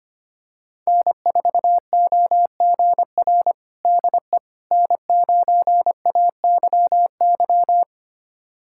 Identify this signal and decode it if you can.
Morse code, heard as a keyed tone, about 25 wpm, 700 Hz. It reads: N4OGR DE N9AYY
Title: USA Callsign DE Callsign - 5 Characters - 25wpm